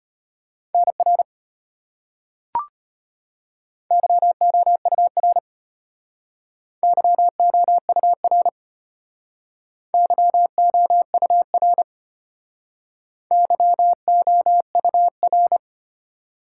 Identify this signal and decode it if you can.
Morse code, heard as a keyed tone, about 33 wpm, 700 Hz. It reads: NR YOUR YOUR YOUR YOUR